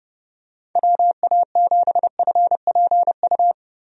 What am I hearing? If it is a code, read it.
Morse code, heard as a keyed tone, about 30 wpm, 700 Hz. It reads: WA7FPU
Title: USA Call Signs - 6 Characters - 30wpm